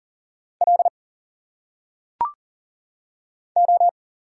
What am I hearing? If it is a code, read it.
Morse code, heard as a keyed tone, about 40 wpm, 700 Hz. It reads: L O
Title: Single Letters - Phonetic Pronunciation 40wpm